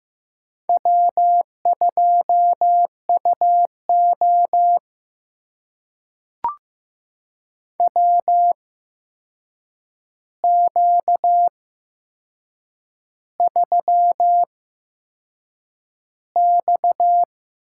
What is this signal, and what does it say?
Morse code, heard as a keyed tone, about 15 wpm, spaced as for 6 wpm, 700 Hz. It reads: W2UO WQ3X
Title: USA Call Signs - 4 Characters - 8x Character Spacing - 15wpm